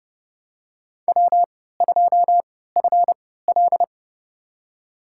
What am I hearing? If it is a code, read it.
Morse code, heard as a keyed tone, about 30 wpm, spaced as for 17 wpm, 700 Hz. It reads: W2FL
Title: USA Call Signs - 4 Characters - 3x Character Spacing - 30wpm